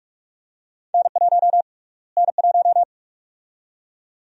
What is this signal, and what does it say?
Morse code, heard as a keyed tone, about 46 wpm, 700 Hz. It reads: N1 N1
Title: USA Call Sign Prefixes - Repeated 2x - 45wpm